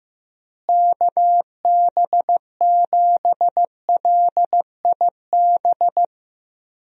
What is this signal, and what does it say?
Morse code, heard as a keyed tone, about 15 wpm, 700 Hz. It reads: KB7LIB
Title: USA Call Signs - 6 Characters - Repeated 2x - 15wpm